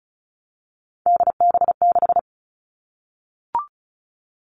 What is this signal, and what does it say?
Morse code, heard as a keyed tone, about 36 wpm, 700 Hz. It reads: DB6
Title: Three Challenging Letter-Number Combinations 35wpm